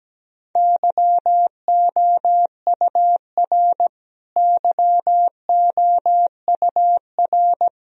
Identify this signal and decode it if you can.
Morse code, heard as a keyed tone, about 17 wpm, 700 Hz. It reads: YOUR YOUR
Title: Top 100 Words Repeated 17wpm